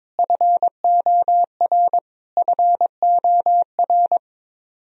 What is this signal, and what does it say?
Morse code, heard as a keyed tone, about 22 wpm, 700 Hz. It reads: FOR FOR
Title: Top 100 Words Repeated 22wpm